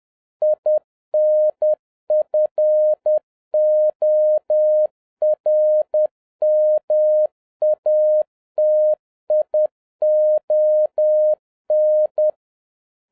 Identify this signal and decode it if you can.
Morse code, heard as a keyed tone, about 10 wpm, 610 Hz. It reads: INFORMATION